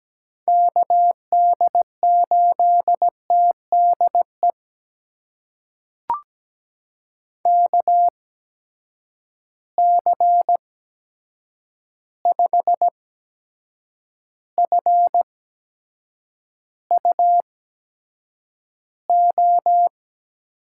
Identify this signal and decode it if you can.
Morse code, heard as a keyed tone, about 17 wpm, spaced as for 7 wpm, 700 Hz. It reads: KD8TDE KC5FUO
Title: USA Call Signs - 6 Characters - 8x Character Spacing - 17wpm